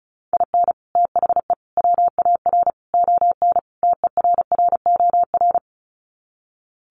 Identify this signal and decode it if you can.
Morse code, heard as a keyed tone, about 34 wpm, 700 Hz. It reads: IN THE WAR ON TERROR
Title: Sets of 5 Words Encore2 35wpm